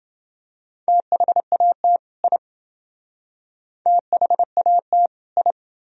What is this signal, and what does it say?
Morse code, heard as a keyed tone, about 28 wpm, 700 Hz. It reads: THAT I THAT I